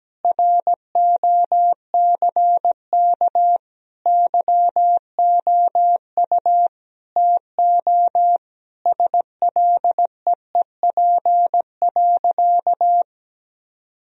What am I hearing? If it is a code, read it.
Morse code, heard as a keyed tone, about 17 wpm, 700 Hz. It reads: ROCK YOU TO SLEEP.
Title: Sentences from Top 1000 Words - Review Part 2 - 17wpm